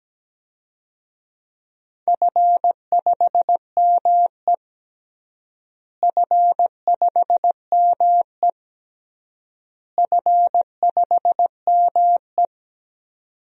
Morse code, heard as a keyed tone, about 17 wpm, 700 Hz. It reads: F5ME F5ME F5ME
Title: DXCC-100 Call Signs - 4 Characters - Repeated 3x - 17wpm